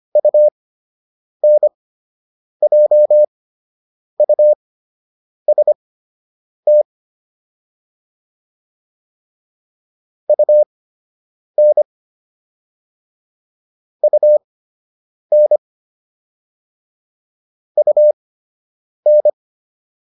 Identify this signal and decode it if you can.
Morse code, heard as a keyed tone, about 25 wpm, spaced as for 8 wpm, 600 Hz. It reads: UNJUST UN UN UN